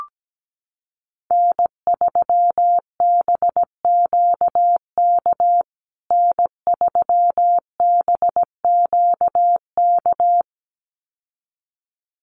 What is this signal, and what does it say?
Morse code, heard as a keyed tone, about 17 wpm, 700 Hz. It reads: N3BQK N3BQK
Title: USA Call Signs Repeated 2x 17wpm